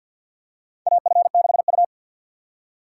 Morse code, heard as a keyed tone, about 45 wpm, 700 Hz. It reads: AF6U
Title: USA Call Signs - 4 Characters - 6x Character Spacing - 50wpm